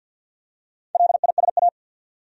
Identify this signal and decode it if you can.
Morse code, heard as a keyed tone, about 50 wpm, 700 Hz. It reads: LISA
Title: USA Common Names - Max 5 Letters - 50wpm